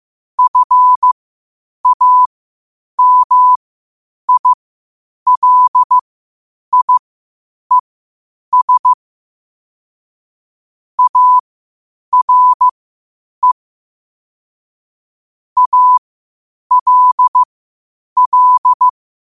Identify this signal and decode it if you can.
Morse code, heard as a keyed tone, about 15 wpm, spaced as for 8 wpm, 1000 Hz. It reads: FAMILIES ARE ALL